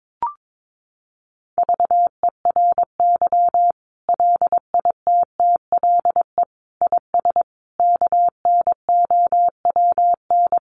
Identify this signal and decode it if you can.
Morse code, heard as a keyed tone, about 22 wpm, 700 Hz. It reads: VERY LITTLE IS KNOWN
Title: Sentences from Top 1500 Words 22wpm